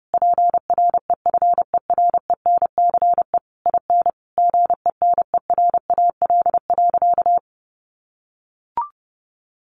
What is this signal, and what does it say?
Morse code, heard as a keyed tone, about 30 wpm, 700 Hz. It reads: PREFERENCE IN GENERAL.
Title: Sentences from Top 1500 Words 30wpm